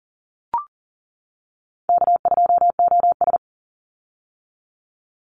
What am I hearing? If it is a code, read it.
Morse code, heard as a keyed tone, about 40 wpm, 700 Hz. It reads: K2OS
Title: Easy Call Signs - 40wpm